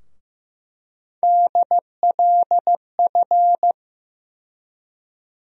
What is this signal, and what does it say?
Morse code, heard as a keyed tone, about 15 wpm, 700 Hz. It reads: DLF
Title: USA Call Sign Postfixes - 3 Letters 15wpm